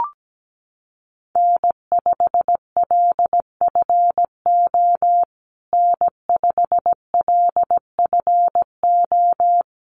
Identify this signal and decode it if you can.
Morse code, heard as a keyed tone, about 17 wpm, 700 Hz. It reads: N5LFO N5LFO
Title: USA Call Signs Repeated 3x 17wpm